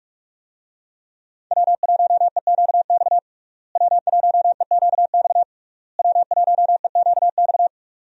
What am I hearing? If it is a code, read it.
Morse code, heard as a keyed tone, about 41 wpm, 700 Hz. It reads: W1EQX W1EQX W1EQX
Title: USA Call Signs - 5 Characters - Repeated 3x - 45wpm